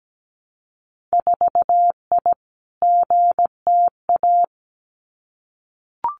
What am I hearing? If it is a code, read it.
Morse code, heard as a keyed tone, about 17 wpm, 700 Hz. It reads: 4I GTA
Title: Winter Field Day Exchanges Encore 17wpm